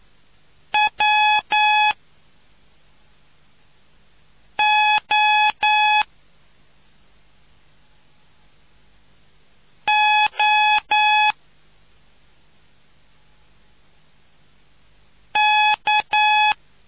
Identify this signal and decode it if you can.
Morse code, heard as a keyed tone, about 9 wpm, 850 Hz. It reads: W O O K